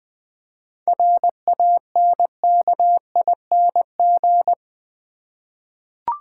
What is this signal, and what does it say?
Morse code, heard as a keyed tone, about 20 wpm, 700 Hz. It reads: RANKING
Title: 7 Letter Words - First 3 Letters Given - 20wpm